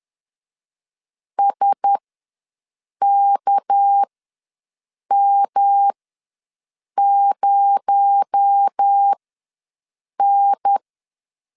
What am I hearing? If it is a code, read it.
Morse code, heard as a keyed tone, about 10 wpm, spaced as for 6 wpm, 800 Hz. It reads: SKM0N
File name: LEZIONI / 7 GF9Z / 7 GF ws 10wpm 3918